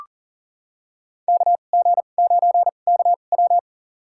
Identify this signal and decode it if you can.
Morse code, heard as a keyed tone, about 40 wpm, spaced as for 29 wpm, 700 Hz. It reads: KG9KW
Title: USA Call Signs - 5 Characters - 2x Character Spacing - 40wpm